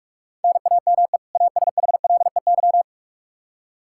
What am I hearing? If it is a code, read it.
Morse code, heard as a keyed tone, about 43 wpm, 700 Hz. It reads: NAME ASHLEY
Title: Practice QSO - Exchange 1 - Beginning - 45wpm